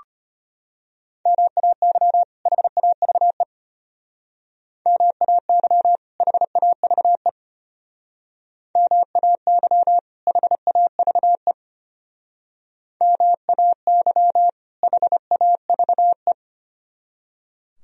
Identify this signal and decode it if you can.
Morse code, heard as a keyed tone, about 33 wpm, 700 Hz. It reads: MAY HAVE MAY HAVE MAY HAVE MAY HAVE